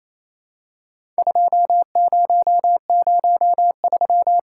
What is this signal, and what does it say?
Morse code, heard as a keyed tone, about 28 wpm, 700 Hz. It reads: 2003
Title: Worked All Europe DX Contest - QTC Exchange - 28wpm